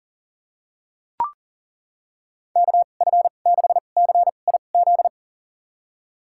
Morse code, heard as a keyed tone, about 40 wpm, spaced as for 29 wpm, 700 Hz. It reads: KF6CIZ
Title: USA Call Signs - 6 Characters - 2x Character Spacing - 40wpm